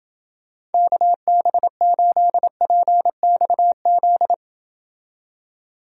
Morse code, heard as a keyed tone, about 29 wpm, 700 Hz. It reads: KB8PXZ